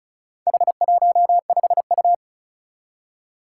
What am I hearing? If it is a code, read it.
Morse code, heard as a keyed tone, about 36 wpm, 700 Hz. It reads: H15U
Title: Four Challenging Letter-Number Combinations 35wpm